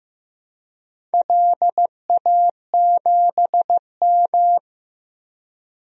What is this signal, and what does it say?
Morse code, heard as a keyed tone, about 15 wpm, 700 Hz. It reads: LA7M
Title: DXCC-100 Call Signs - 4 Characters - 15wpm